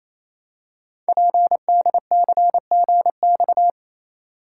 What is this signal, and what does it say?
Morse code, heard as a keyed tone, about 27 wpm, 700 Hz. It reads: PDCGX
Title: Code Groups - 5 Letters - 28wpm